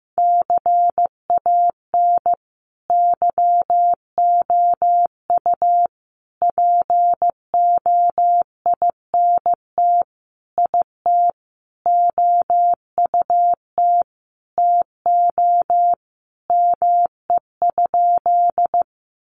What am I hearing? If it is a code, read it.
Morse code, heard as a keyed tone, about 14 wpm, 700 Hz. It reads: CAN YOU POINT IT OUT TO ME?
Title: Sentences from Top 200 Words 15wpm